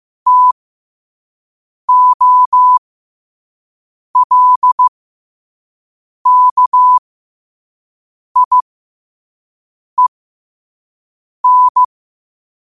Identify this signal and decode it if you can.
Morse code, heard as a keyed tone, about 15 wpm, spaced as for 5 wpm, 1000 Hz. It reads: TOLKIEN
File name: qotdfiles / 5wpm